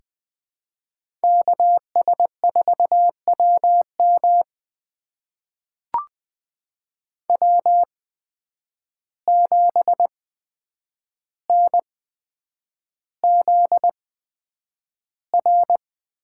Morse code, heard as a keyed tone, about 20 wpm, spaced as for 9 wpm, 700 Hz. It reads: KS4WM W7NZR
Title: USA Call Signs - 8x Character Spacing - 20wpm